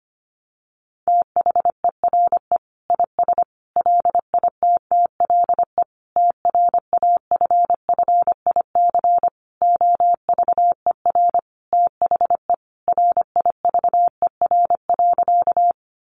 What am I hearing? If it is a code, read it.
Morse code, heard as a keyed tone, about 25 wpm, 700 Hz. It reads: THERE IS LITTLE TRAFFIC OVER THE RIVER.
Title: Sentences from Top 1300 Words 25wpm